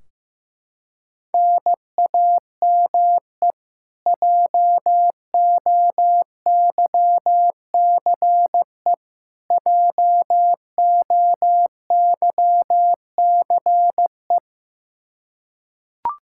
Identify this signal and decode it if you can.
Morse code, heard as a keyed tone, about 15 wpm, 700 Hz. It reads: NAME JOYCE JOYCE
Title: Practice QSO - Exchange 1 - Beginning - 15wpm